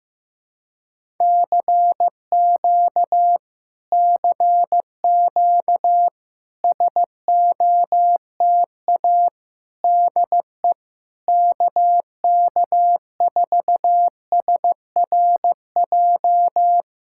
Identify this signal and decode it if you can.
Morse code, heard as a keyed tone, about 15 wpm, 700 Hz. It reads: CQ CQ SOTA DE KK4SRJ
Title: Speed-Racing Common QSO Phrases Encore - 15wpm